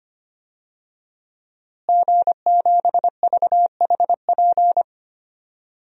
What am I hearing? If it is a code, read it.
Morse code, heard as a keyed tone, about 25 wpm, 700 Hz. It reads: G7VHP
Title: DXCC-100 Call Signs - 5 Characters - Repeated 2x - 25wpm